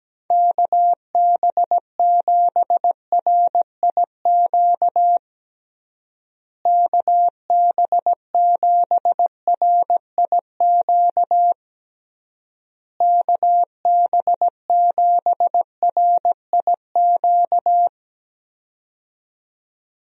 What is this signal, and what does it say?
Morse code, heard as a keyed tone, about 17 wpm, 700 Hz. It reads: KB7RIQ KB7RIQ KB7RIQ
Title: USA Call Signs - 6 Characters - Repeated 3x - 17wpm